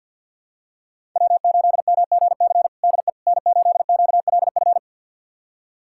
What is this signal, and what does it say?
Morse code, heard as a keyed tone, about 52 wpm, 700 Hz. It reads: W8MGC DE N8QLF
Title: USA Callsign DE Callsign - 5 Characters - 50wpm